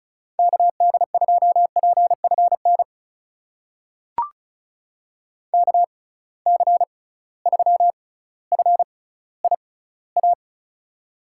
Morse code, heard as a keyed tone, about 36 wpm, spaced as for 18 wpm, 700 Hz. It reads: KD2PFN KC3FIA
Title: USA Call Signs - 6 Characters - 6x Character Spacing - 35wpm